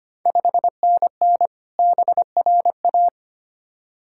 Morse code, heard as a keyed tone, about 25 wpm, 700 Hz. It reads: 5NN BRA
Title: Ontario QSO Party Exchange 25wpm